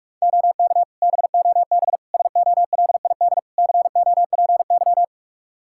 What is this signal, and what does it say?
Morse code, heard as a keyed tone, about 42 wpm, 700 Hz. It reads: OK BOB SOLID COPY
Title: Practice QSO - Exchange 3 - More Info - 45wpm